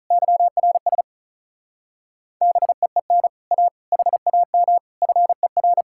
Morse code, heard as a keyed tone, about 45 wpm, 700 Hz. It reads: YRS BEEN A HAM FER